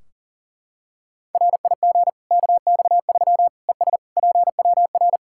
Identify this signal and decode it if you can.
Morse code, heard as a keyed tone, about 40 wpm, 700 Hz. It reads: RIG KX3 ES PWR
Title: Practice QSO - Exchange 2 - Info - 40wpm